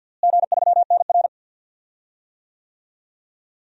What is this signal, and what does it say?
Morse code, heard as a keyed tone, about 50 wpm, 700 Hz. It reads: G3NR
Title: DXCC-100 Call Signs - 4 Characters - Repeated 3x - 50wpm